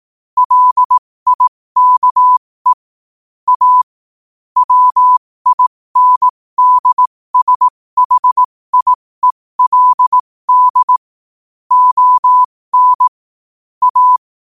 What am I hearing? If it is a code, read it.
Morse code, heard as a keyed tone, about 18 wpm, spaced as for 15 wpm, 1000 Hz. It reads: LIKE A WINDSHIELD ON A
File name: qotdfiles / 15wpm